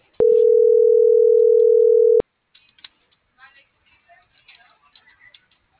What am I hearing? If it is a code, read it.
Ringback tone (North America), 440 + 480 Hz.
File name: CENTRAL MORENA / out jacque.aguilera.25